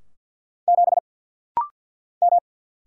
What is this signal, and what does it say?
Morse code, heard as a keyed tone, about 50 wpm, 700 Hz. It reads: 7 M
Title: Single Letter-Number - Rapid-Fire 50wpm